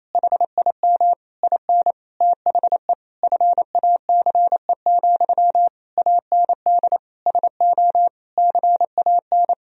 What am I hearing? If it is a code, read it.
Morse code, heard as a keyed tone, about 28 wpm, 700 Hz. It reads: HIM IN THE FACE, AND SO CAN
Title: Sentences from Top 1000 Words - Review Part 2 - 28wpm